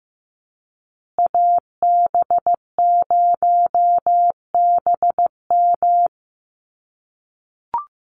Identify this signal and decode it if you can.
Morse code, heard as a keyed tone, about 15 wpm, 700 Hz. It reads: AB0BM
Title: USA Call Signs Repeated 2x 15wpm